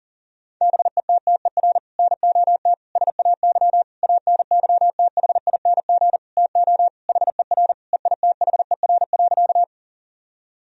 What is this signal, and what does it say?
Morse code, heard as a keyed tone, about 40 wpm, 700 Hz. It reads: BETTER NOT SAY ANYTHING TO HER EITHER.
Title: Sentences from Top 1000 Words - Review Part 4 - 40wpm